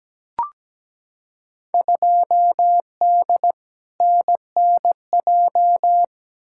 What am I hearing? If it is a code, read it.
Morse code, heard as a keyed tone, about 17 wpm, 700 Hz. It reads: 2D NNJ
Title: Field Day Exchanges - 17wpm